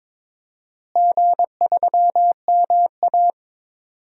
Morse code, heard as a keyed tone, about 22 wpm, 700 Hz. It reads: G3MA
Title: DXCC-100 Call Signs - 4 Characters - Repeated 2x - 22wpm